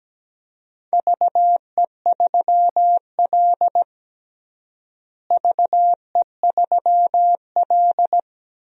Morse code, heard as a keyed tone, about 17 wpm, 700 Hz. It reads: VE3L VE3L
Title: DXCC-100 Call Signs - 4 Characters - Repeated 3x - 17wpm